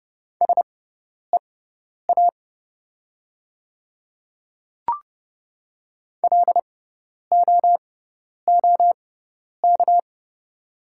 Morse code, heard as a keyed tone, about 30 wpm, spaced as for 10 wpm, 700 Hz. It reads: SEA LOOK
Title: 6x Character Spacing - Top 300 Words - 30wpm